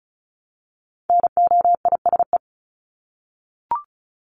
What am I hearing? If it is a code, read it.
Morse code, heard as a keyed tone, about 34 wpm, 700 Hz. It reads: NOISE
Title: Top 1501-1750 Words 35wpm